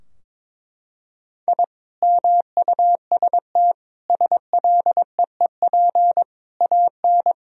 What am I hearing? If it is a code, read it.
Morse code, heard as a keyed tone, about 22 wpm, 700 Hz. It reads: I MUST SLEEP AN
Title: Sentences from Top 1000 Words - Review Part 4 - 22wpm